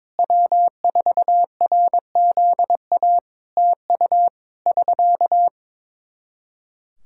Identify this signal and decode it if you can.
Morse code, heard as a keyed tone, about 22 wpm, 700 Hz. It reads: W4RZA TU <SK>
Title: Practice QSO - Exchange 4 - Ending - 22wpm